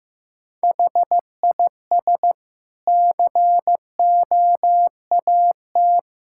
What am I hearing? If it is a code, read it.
Morse code, heard as a keyed tone, about 15 wpm, 700 Hz. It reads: HIS COAT
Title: Sentences from Top 1600 Words 15wpm